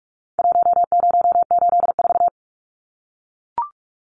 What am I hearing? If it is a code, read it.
Morse code, heard as a keyed tone, about 44 wpm, 700 Hz. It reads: 1084
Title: Numbers - Contest Style - 4-Digit 45wpm